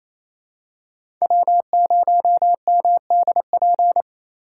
Morse code, heard as a keyed tone, about 29 wpm, 700 Hz. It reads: W0MDP
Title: USA Call Signs - 5 Characters - 2x Character Spacing - 28wpm